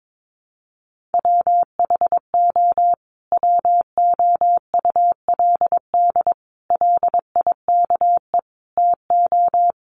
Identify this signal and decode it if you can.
Morse code, heard as a keyed tone, about 22 wpm, 700 Hz. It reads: WHO WOULD LIKE TO
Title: Sentences from Top 1500 Words 22wpm